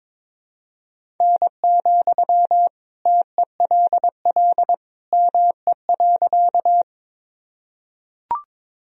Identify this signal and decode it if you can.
Morse code, heard as a keyed tone, about 22 wpm, 700 Hz. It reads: N, TELL ME.
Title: Sentences from Top 200 Words and Abbreviations 22wpm